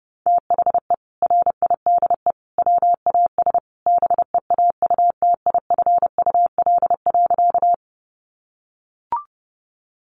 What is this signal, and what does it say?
Morse code, heard as a keyed tone, about 30 wpm, 700 Hz. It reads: THE RIDE WAS BEAUTIFUL.
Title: Sentences from Top 1200 Words 30wpm